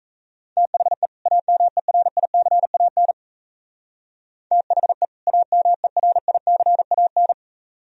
Morse code, heard as a keyed tone, about 37 wpm, 700 Hz. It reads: THE AMERICAN THE AMERICAN